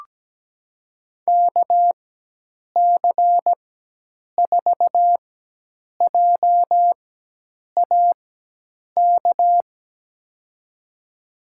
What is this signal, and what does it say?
Morse code, heard as a keyed tone, about 17 wpm, spaced as for 8 wpm, 700 Hz. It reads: KC4JAK
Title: USA Call Signs - 6 Characters - 4x Character Spacing - 17wpm